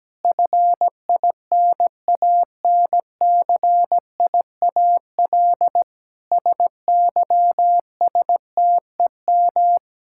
Morse code, heard as a keyed tone, about 17 wpm, 700 Hz. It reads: FINANCIAL SYSTEM
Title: Sentences from Top 1000 Words - Review Part 5 - 17wpm